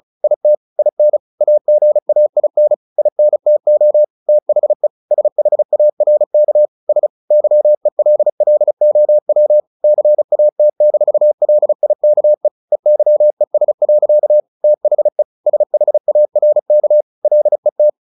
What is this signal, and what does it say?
Morse code, heard as a keyed tone, about 34 wpm, 600 Hz. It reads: IT IN AGAIN INTO THE SHARK S YELLOW CAT-LIKE EYES. THE SHARK LET
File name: the old man and the sea 2 35wpm